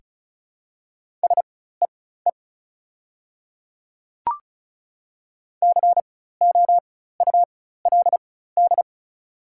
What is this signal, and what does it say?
Morse code, heard as a keyed tone, about 34 wpm, spaced as for 16 wpm, 700 Hz. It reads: SEE COULD
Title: 4x Character Spacing - Top 100 Words - 35wpm